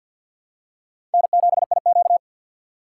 Morse code, heard as a keyed tone, about 48 wpm, 700 Hz. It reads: N7IQ
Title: USA Call Signs - 4 Characters - 6x Character Spacing - 50wpm